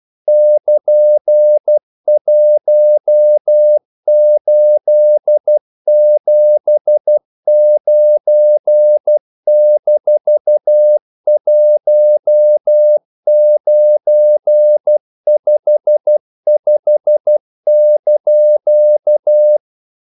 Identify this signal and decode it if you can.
Morse code, heard as a keyed tone, about 12 wpm, 600 Hz. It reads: <KN>1879-1955)
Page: English quotations in CW (telegraphy - morse)